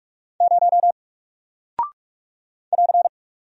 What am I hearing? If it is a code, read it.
Morse code, heard as a keyed tone, about 45 wpm, 700 Hz. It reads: 0 <AR>
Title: Single Character (Letter:Number:Punctuation:Prosign) Alt 45wpm